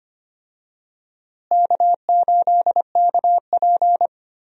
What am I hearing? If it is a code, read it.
Morse code, heard as a keyed tone, about 25 wpm, 700 Hz. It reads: K8KP
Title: USA Call Signs - 4 Characters - Repeated 3x - 25wpm